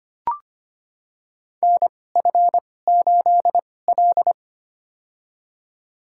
Morse code, heard as a keyed tone, about 25 wpm, spaced as for 18 wpm, 700 Hz. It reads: NF8L
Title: USA Call Signs - 4 Characters - 2x Character Spacing - 25wpm